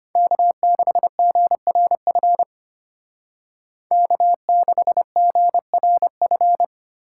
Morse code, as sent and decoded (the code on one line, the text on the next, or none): -.- -.... --. .-. ..-. / -.- -.... --. .-. ..-.
K6GRF K6GRF